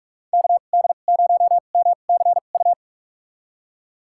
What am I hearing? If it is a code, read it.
Morse code, heard as a keyed tone, about 46 wpm, spaced as for 33 wpm, 700 Hz. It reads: KD0MCU
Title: USA Call Signs - 6 Characters - 2x Character Spacing - 45wpm